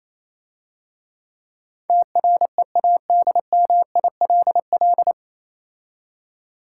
Morse code, heard as a keyed tone, about 27 wpm, 700 Hz. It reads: TREADMILL
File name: Easy Words - Speed-Racing with Hints - 22wpm